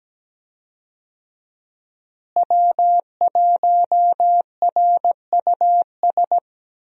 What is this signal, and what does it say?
Morse code, heard as a keyed tone, about 17 wpm, 700 Hz. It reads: W1RUS
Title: USA Call Signs - 5 Characters - Repeated 2x - 17wpm